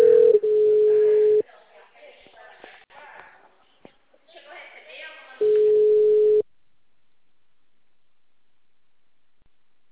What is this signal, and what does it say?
Ringback tone (Europe, 425 Hz), with speech.
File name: out- -2044- - - .22712